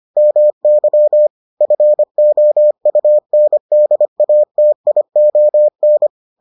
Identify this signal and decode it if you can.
Morse code, heard as a keyed tone, about 25 wpm, 600 Hz. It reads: MY FOUNDATION